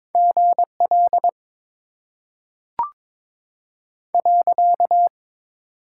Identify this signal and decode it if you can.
Morse code, heard as a keyed tone, about 22 wpm, 700 Hz. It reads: GL .
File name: Top 90 QSO Elements - 6x Character Spacing - 22wpm